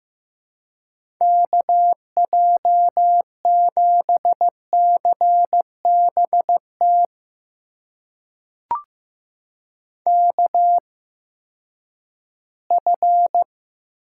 Morse code, heard as a keyed tone, about 15 wpm, spaced as for 10 wpm, 700 Hz. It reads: KJ7CBT KF
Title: USA Call Signs - 6 Characters - 8x Character Spacing - 15wpm